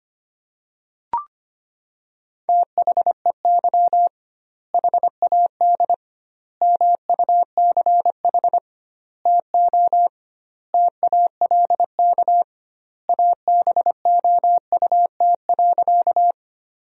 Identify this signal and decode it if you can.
Morse code, heard as a keyed tone, about 25 wpm, 700 Hz. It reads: THEY HAD MUCH TO TALK ABOUT.
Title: 2x Word-Spacing - Sentences from Top 400 Words - 25wpm